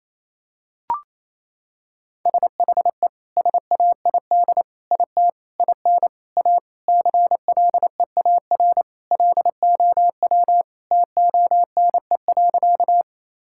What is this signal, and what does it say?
Morse code, heard as a keyed tone, about 27 wpm, 700 Hz. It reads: SHE SAID IT IN A CLEAR LOW TONE.
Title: Sentences from Top 1600 Words 28wpm